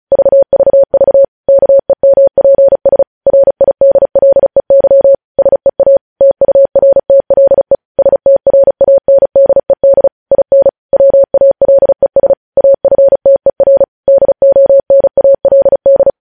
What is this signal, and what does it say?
Morse code, heard as a keyed tone, about 36 wpm, 550 Hz. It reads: VVV KEMPS RIDLEY SEA TURTLE STRANDED IN WALES AFTER DONALD
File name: podcast / News headlines-1x-35x35